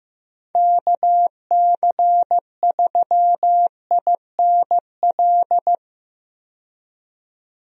Morse code, heard as a keyed tone, about 15 wpm, 700 Hz. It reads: KC3INL
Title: USA Call Signs - 6 Characters - 15wpm